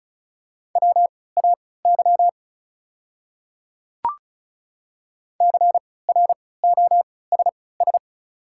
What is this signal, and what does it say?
Morse code, heard as a keyed tone, about 36 wpm, spaced as for 20 wpm, 700 Hz. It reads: WAY CROSS
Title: 3x Character Spacing - Top 300 Words - 35wpm